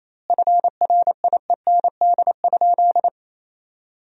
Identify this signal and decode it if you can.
Morse code, heard as a keyed tone, about 28 wpm, 700 Hz. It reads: FRIEND?
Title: 2x Word-Spacing - Sentences from Top 300 Words - 28wpm